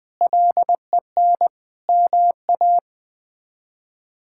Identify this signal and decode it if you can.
Morse code, heard as a keyed tone, about 21 wpm, 700 Hz. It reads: LEN MA
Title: K1USN SST QSO Practice - 20wpm